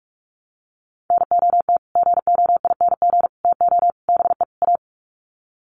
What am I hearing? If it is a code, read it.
Morse code, heard as a keyed tone, about 44 wpm, 700 Hz. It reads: NOT GOING TO BE A
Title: Sets of 5 Words Encore 45wpm